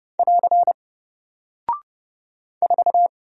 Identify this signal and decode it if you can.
Morse code, heard as a keyed tone, about 30 wpm, 700 Hz. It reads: <AR> 4
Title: Single Character (Letter:Number:Punctuation:Prosign) Alt 30wpm